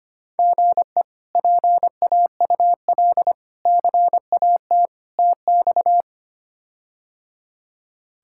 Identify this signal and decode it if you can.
Morse code, heard as a keyed tone, about 25 wpm, 700 Hz. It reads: GE PAUL CAT TX
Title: K1USN SST QSO Practice - 25wpm